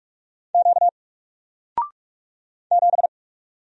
Morse code, heard as a keyed tone, about 46 wpm, 700 Hz. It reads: Q 7
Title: Single Letter-Number 45wpm